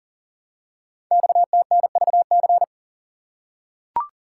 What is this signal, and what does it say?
Morse code, heard as a keyed tone, about 37 wpm, 700 Hz. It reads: XTNVC
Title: Code Groups - 5 Letter:Numbers - 40wpm